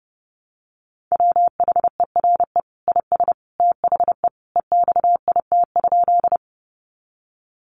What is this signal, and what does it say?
Morse code, heard as a keyed tone, about 30 wpm, 700 Hz. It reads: WHERE IS THE EXIT?
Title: Sentences from Top 1100 Words 30wpm